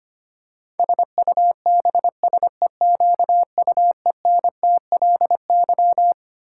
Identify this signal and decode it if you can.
Morse code, heard as a keyed tone, about 25 wpm, 700 Hz. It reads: SUBSEQUENTLY
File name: Twelve Letter Words - 25wpm